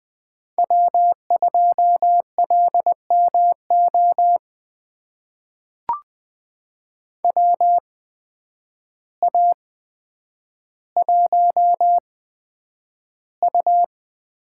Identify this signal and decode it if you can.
Morse code, heard as a keyed tone, about 20 wpm, spaced as for 9 wpm, 700 Hz. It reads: W2LMO WA1U
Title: USA Call Signs - 8x Character Spacing - 20wpm